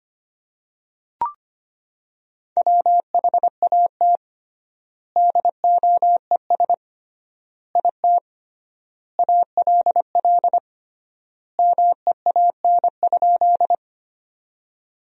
Morse code, heard as a keyed tone, about 25 wpm, 700 Hz. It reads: WHAT DOES IT ALL MEAN?
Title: 3x Word-Spacing - Sentences from Top 200 Words - 25wpm